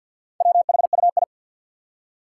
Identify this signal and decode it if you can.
Morse code, heard as a keyed tone, about 51 wpm, 700 Hz. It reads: WHUI
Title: Four Challenging Letter-Number Combinations 50wpm